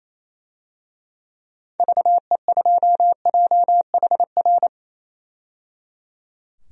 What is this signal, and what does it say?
Morse code, heard as a keyed tone, about 28 wpm, 700 Hz. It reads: VE2JHR
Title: DXCC-100 Call Signs - 6 Characters - 28wpm